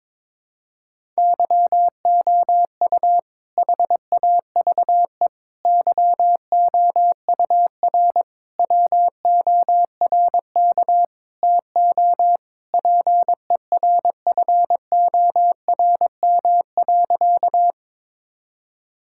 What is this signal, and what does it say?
Morse code, heard as a keyed tone, about 22 wpm, 700 Hz. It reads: YOU HAVE YOUR WORK TO PERFORM.
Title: Sentences from Top 1000 Words - Review Part 5 - 22wpm